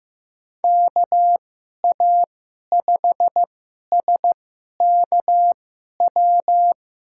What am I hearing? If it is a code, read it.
Morse code, heard as a keyed tone, about 15 wpm, spaced as for 11 wpm, 700 Hz. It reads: KA5SKW
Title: USA Call Signs - 6 Characters - 2x Character Spacing - 15wpm